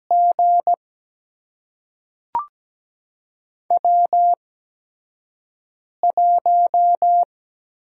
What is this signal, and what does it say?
Morse code, heard as a keyed tone, about 17 wpm, spaced as for 5 wpm, 700 Hz. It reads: G W1
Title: USA Call Signs - 4 Characters - 8x Character Spacing - 17wpm